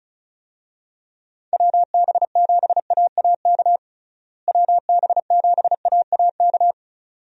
Morse code, heard as a keyed tone, about 36 wpm, 700 Hz. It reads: WB7AAK WB7AAK
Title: USA Call Signs - 6 Characters - Repeated 2x - 35wpm